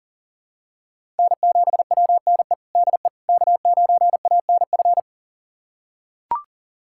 Morse code, heard as a keyed tone, about 37 wpm, 700 Hz. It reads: N7WNE DE K9ANF
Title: USA Callsign DE Callsign - 5 Characters - 40wpm